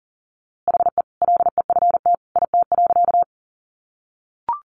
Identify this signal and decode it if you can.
Morse code, heard as a keyed tone, about 40 wpm, 700 Hz. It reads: HE LEFT IT.
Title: A Sound Of Thunder 40wpm